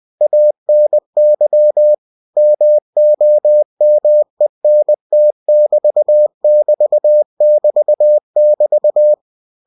Morse code, heard as a keyed tone, about 20 wpm, 600 Hz. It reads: ANY MOMENT<BT><BT><BT><BT>
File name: cw-books / winnie-the-pooh / winnie-20wpm-page-0097